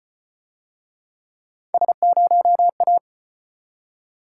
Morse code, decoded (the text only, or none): S0A